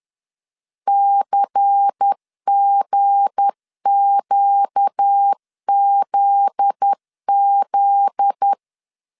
Morse code, heard as a keyed tone, about 10 wpm, 800 Hz. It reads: CGQZZ
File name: LEZIONI / 7 GF9Z / 7 GF9Z 10wpm 2611